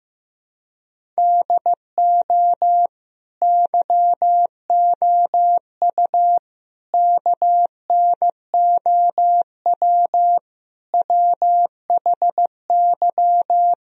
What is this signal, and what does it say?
Morse code, heard as a keyed tone, about 15 wpm, 700 Hz. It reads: DO YOU KNOW WHY
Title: Sentences from Top 1000 Words - Review Part 1 - 15wpm